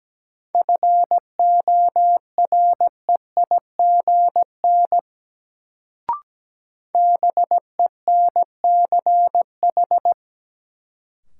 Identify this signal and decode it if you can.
Morse code, heard as a keyed tone, about 17 wpm, 700 Hz. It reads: FOREIGN BENCH
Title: Sentences from Top 1400 Words - New Word Drill Alt - 17wpm